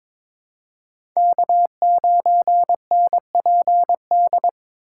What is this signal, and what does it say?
Morse code, heard as a keyed tone, about 22 wpm, 700 Hz. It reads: K9NPD
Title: USA Call Signs - 5 Characters - 4x Character Spacing - 22wpm